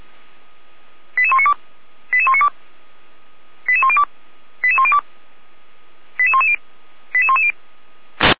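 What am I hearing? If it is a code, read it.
Five-tone selective call (ZVEI), heard as quick runs of five tones, tones of about 70 ms. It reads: 88192 88192 88102 88102 88110 88110
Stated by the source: ZVEI five-tone selective call received via SDR.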